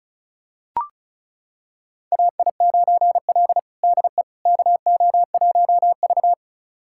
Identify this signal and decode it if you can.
Morse code, heard as a keyed tone, about 34 wpm, 700 Hz. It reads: AI9L DE KO1V
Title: USA Callsign DE Callsign - 4 Characters - 35wpm